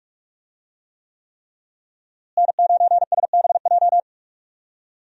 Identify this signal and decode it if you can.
Morse code, heard as a keyed tone, about 45 wpm, 700 Hz. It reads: N9SBJ
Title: USA Call Signs - 5 Characters - Repeated 3x - 45wpm